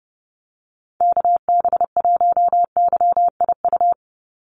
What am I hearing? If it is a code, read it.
Morse code, heard as a keyed tone, about 30 wpm, 700 Hz. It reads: KB1YIU
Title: USA Call Signs 30wpm